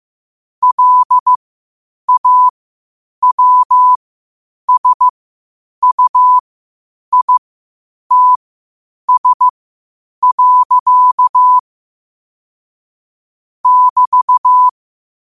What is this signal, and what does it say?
Morse code, heard as a keyed tone, about 15 wpm, spaced as for 8 wpm, 1000 Hz. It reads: LAWSUITS. <BT>
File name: qotdfiles / 8wpm